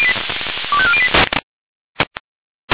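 Five-tone selective call (ZVEI), heard as quick runs of five tones, tones of about 70 ms. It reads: 25408
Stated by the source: ZVEI five-tone selective call received with an SDR.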